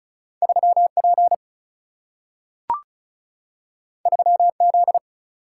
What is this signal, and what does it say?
Morse code, heard as a keyed tone, about 36 wpm, 700 Hz. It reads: 3P 3Z
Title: Single-Letter Single-Number 35wpm